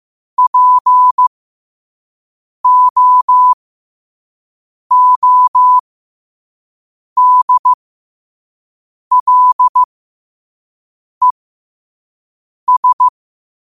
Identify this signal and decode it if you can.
Morse code, heard as a keyed tone, about 15 wpm, spaced as for 5 wpm, 1000 Hz. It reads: POODLES
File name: qotdfiles / 5wpm